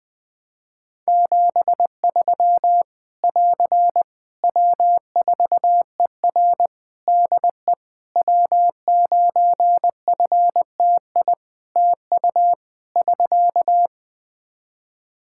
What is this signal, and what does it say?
Morse code, heard as a keyed tone, about 20 wpm, 700 Hz. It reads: 73 <AR> W4ER DE W9FTI TU <SK>
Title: Practice QSO - Exchange 4 - Ending - 20wpm